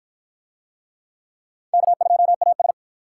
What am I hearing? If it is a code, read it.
Morse code, heard as a keyed tone, about 50 wpm, 700 Hz. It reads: K2AS